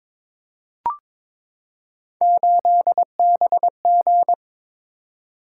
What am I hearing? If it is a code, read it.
Morse code, heard as a keyed tone, about 22 wpm, 700 Hz. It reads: 8BG
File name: practice / 22wpm / Three Letter-Number Combinations - 22wpm